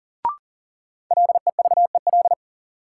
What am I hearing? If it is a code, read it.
Morse code, heard as a keyed tone, about 37 wpm, 700 Hz. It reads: LEVEL
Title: Sentences from Top 700 Words - New Word Drill - 40wpm